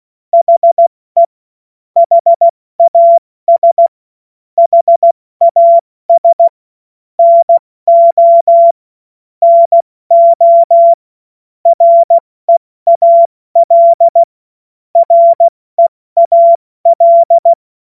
Morse code, heard as a keyed tone, about 16 wpm, 680 Hz. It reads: HE HAS HAS NO NO REAL REAL